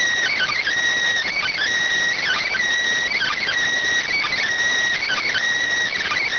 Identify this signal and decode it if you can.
Emergency Alert System (SAME) tones (header bursts), heard as harsh buzzing bursts. It is heard from 1.2 to 5.9 s.